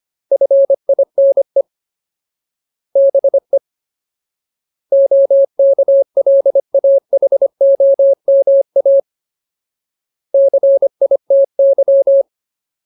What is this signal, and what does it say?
Morse code, heard as a keyed tone, about 25 wpm, 550 Hz. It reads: FINE BE OKLAHOMA CITY
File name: Ultimate Word List / 25WPM Ultimate